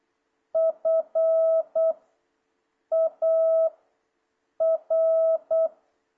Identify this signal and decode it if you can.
Morse code, heard as a keyed tone, about 8 wpm, spaced as for 6 wpm, 650 Hz. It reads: FAR
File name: jh8cbh / 2026cw